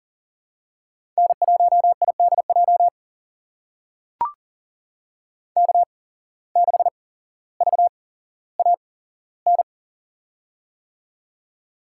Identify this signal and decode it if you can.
Morse code, heard as a keyed tone, about 40 wpm, spaced as for 17 wpm, 700 Hz. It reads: N1IDJ K6VAN
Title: USA Call Signs - 5 Characters - 8x Character Spacing - 40wpm